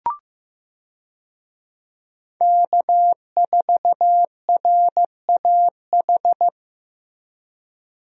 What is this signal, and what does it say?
Morse code, heard as a keyed tone, about 15 wpm, 700 Hz. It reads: K4RAH
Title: USA Call Signs - 5 Characters - Repeated 3x - 15wpm